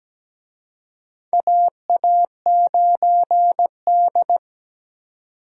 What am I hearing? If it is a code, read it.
Morse code, heard as a keyed tone, about 17 wpm, 700 Hz. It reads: AA9D
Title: USA Call Signs - 4 Characters - 17wpm